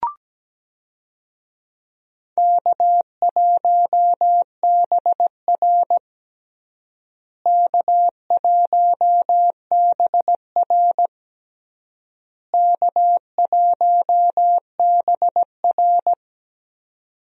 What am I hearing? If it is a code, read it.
Morse code, heard as a keyed tone, about 17 wpm, 700 Hz. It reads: K1BR K1BR K1BR
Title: USA Call Signs - 4 Characters - Repeated 3x - 17wpm